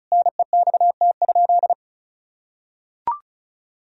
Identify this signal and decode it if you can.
Morse code, heard as a keyed tone, about 36 wpm, 700 Hz. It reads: NEXT?
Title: Sentences from Top 1000 Words - Review Part 2 - 35wpm